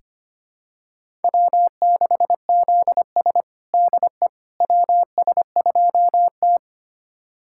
Morse code, heard as a keyed tone, about 25 wpm, 700 Hz. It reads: W6ZS DE WS2T
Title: USA Callsign DE Callsign - 4 Characters - 25wpm